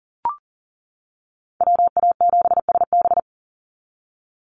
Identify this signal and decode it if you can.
Morse code, heard as a keyed tone, about 40 wpm, 700 Hz. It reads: WA7SB
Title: NA SOTA Chaser Callsigns 40wpm